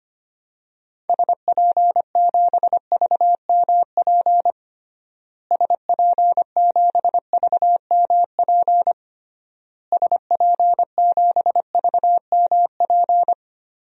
Morse code, heard as a keyed tone, about 25 wpm, 700 Hz. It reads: SP7VMP SP7VMP SP7VMP
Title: DXCC-100 Call Signs - 6 Characters - Repeated 3x - 25wpm